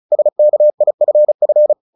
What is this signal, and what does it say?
Morse code, heard as a keyed tone, about 34 wpm, 600 Hz. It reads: SKIFF
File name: the old man and the sea / the old man and the sea 1 35wpm